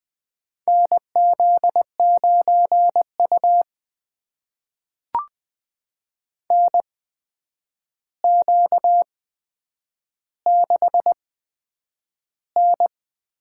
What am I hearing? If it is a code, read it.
Morse code, heard as a keyed tone, about 20 wpm, spaced as for 9 wpm, 700 Hz. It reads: NZ9U NQ6N
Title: USA Call Signs - 8x Character Spacing - 20wpm